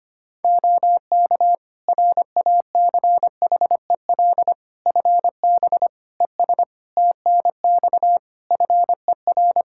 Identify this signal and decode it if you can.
Morse code, heard as a keyed tone, about 25 wpm, 700 Hz. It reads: OK RACHEL FB ES TNX FER
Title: Practice QSO - Exchange 2 - Info - 25wpm